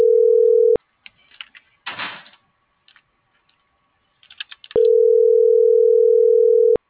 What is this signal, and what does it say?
Ringback tone (North America), 440 + 480 Hz.